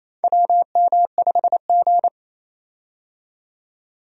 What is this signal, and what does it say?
Morse code, heard as a keyed tone, about 28 wpm, 700 Hz. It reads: WM5G
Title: USA Call Signs - 4 Characters - 28wpm